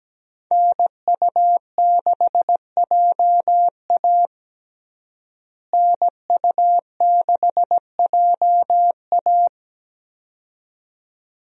Morse code, heard as a keyed tone, about 17 wpm, 700 Hz. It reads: NU6JA NU6JA
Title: USA Call Signs - 5 Characters - Repeated 2x - 17wpm